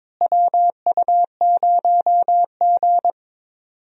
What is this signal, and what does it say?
Morse code, heard as a keyed tone, about 22 wpm, 700 Hz. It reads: WU0G
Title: USA Call Signs - 4 Characters - Repeated 2x - 22wpm